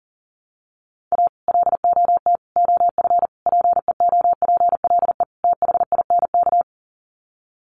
Morse code, heard as a keyed tone, about 37 wpm, 700 Hz. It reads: A LOT OF PEOPLE THINK
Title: Sets of 5 Words Encore 40wpm